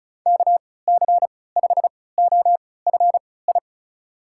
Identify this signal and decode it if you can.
Morse code, heard as a keyed tone, about 34 wpm, spaced as for 20 wpm, 700 Hz. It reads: KC5OFI
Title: USA Call Signs - 6 Characters - 3x Character Spacing - 35wpm